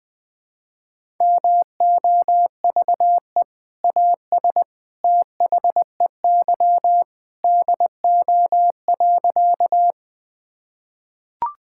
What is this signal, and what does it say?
Morse code, heard as a keyed tone, about 20 wpm, 700 Hz. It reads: MOVE AS THEY DO.
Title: Sentences with 2-Word Phrases - 20wpm